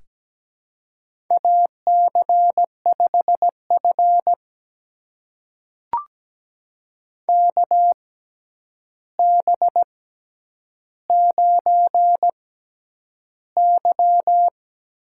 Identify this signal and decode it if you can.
Morse code, heard as a keyed tone, about 17 wpm, spaced as for 9 wpm, 700 Hz. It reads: AC5F KB9Y
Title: USA Call Signs - 4 Characters - 6x Character Spacing - 17wpm